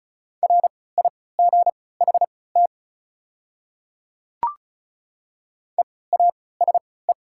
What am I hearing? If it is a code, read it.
Morse code, heard as a keyed tone, about 34 wpm, spaced as for 20 wpm, 700 Hz. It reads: RIGHT EASE
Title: 3x Character Spacing - Top 300 Words - 35wpm